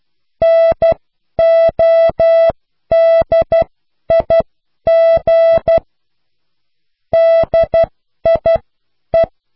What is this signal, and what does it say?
Morse code, heard as a keyed tone, about 12 wpm, spaced as for 10 wpm, 660 Hz. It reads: NODIG DIE